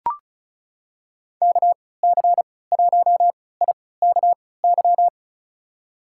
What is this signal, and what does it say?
Morse code, heard as a keyed tone, about 36 wpm, spaced as for 20 wpm, 700 Hz. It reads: KC1IKY
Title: USA Call Signs - 6 Characters - 3x Character Spacing - 35wpm